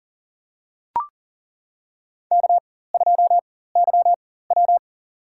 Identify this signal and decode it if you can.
Morse code, heard as a keyed tone, about 40 wpm, spaced as for 19 wpm, 700 Hz. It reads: K2YW
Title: USA Call Signs - 4x Character Spacing - 40wpm